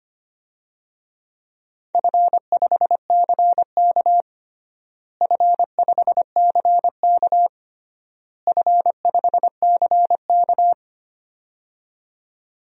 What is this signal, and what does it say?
Morse code, heard as a keyed tone, about 25 wpm, 700 Hz. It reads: F5CK F5CK F5CK
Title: DXCC-100 Call Signs - 4 Characters - Repeated 3x - 25wpm